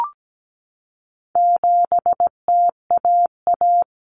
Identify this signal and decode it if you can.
Morse code, heard as a keyed tone, about 17 wpm, 700 Hz. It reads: 7TAA
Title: Common Cut Numbers - 4-Digit 17wpm